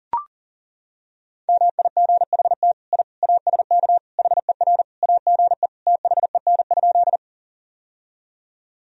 Morse code, heard as a keyed tone, about 37 wpm, 700 Hz. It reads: MIGHT I ASK HER AGE THEN?
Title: Sentences from Top 1000 Words - Review Part 3 - 40wpm